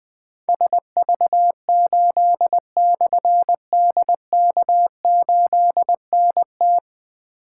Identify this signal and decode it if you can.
Morse code, heard as a keyed tone, about 20 wpm, 700 Hz. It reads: SV8/DK8NT
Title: Difficult Call Signs 20wpm